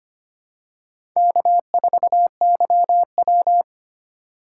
Morse code, heard as a keyed tone, about 25 wpm, 700 Hz. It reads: K4YW
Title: USA Call Signs - 4 Characters - 3x Character Spacing - 25wpm